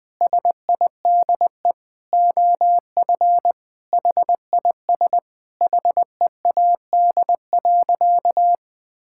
Morse code, heard as a keyed tone, about 20 wpm, 700 Hz. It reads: SIDE OF HIS HEAD.
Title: Sentences from Top 1600 Words 20wpm